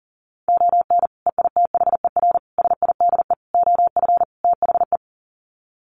Morse code, heard as a keyed tone, about 40 wpm, 700 Hz. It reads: ON EITHER SIDE OF THE
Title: Sets of 5 Words 40wpm